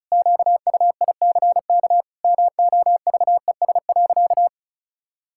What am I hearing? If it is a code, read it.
Morse code, heard as a keyed tone, about 36 wpm, 700 Hz. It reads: QUICK MOVES.
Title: Pangram Challenge v2 - 35wpm